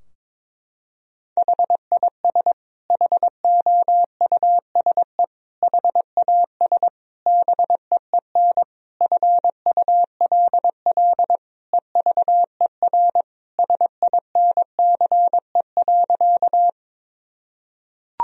Morse code, heard as a keyed tone, about 21 wpm, 700 Hz. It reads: HIS HOUSE HAS BEEN FULL EVER SINCE.
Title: Sentences from Top 1000 Words - Review Part 3 - 22wpm